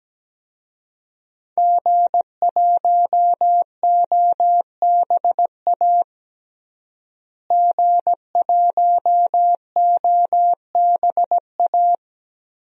Morse code, heard as a keyed tone, about 17 wpm, 700 Hz. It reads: G1OBA G1OBA
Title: DXCC-100 Call Signs - 5 Characters - Repeated 3x - 17wpm